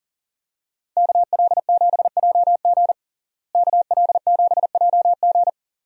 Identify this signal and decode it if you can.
Morse code, heard as a keyed tone, about 40 wpm, 700 Hz. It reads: KL7JG KL7JG
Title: DXCC-100 Call Signs - 5 Characters - Repeated 2x - 40wpm